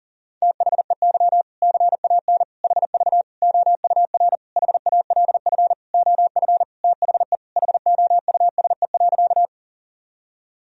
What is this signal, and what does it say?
Morse code, heard as a keyed tone, about 40 wpm, 700 Hz. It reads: THEY CAN HV OUR HALF OF THE HOUSE.
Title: Sentences from Top 400 Words and Abbreviations 40wpm